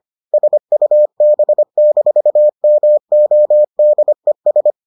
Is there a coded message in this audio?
Yes: Morse code from 0.3 to 4.7 s, heard as a keyed tone, about 25 wpm, 600 Hz. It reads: SUB-MODES